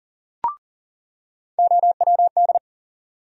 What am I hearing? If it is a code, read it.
Morse code, heard as a keyed tone, about 40 wpm, 700 Hz. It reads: OWD
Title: USA Call Sign Postfixes - 3 Letters 40wpm